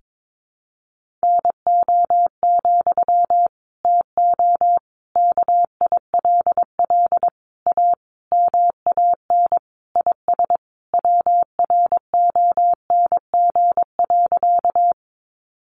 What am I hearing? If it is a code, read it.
Morse code, heard as a keyed tone, about 22 wpm, 700 Hz. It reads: NO, TO KILL A MAN IS WRONG.
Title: Sentences from Top 1400 Words 22wpm